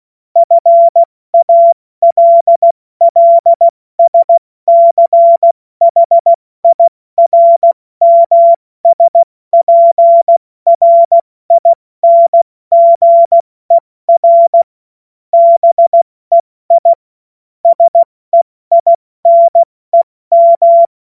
Morse code, heard as a keyed tone, about 16 wpm, 680 Hz. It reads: FALLSCHIRMSPRINGER BEI SEINEM
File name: Morsekurs / Witze / Witz 02. Dezember 2025 70 F80